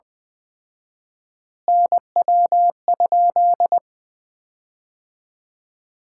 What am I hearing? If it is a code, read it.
Morse code, heard as a keyed tone, about 21 wpm, 700 Hz. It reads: NW?
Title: Common QSO Phrases - 4x Word-Spacing - 20wpm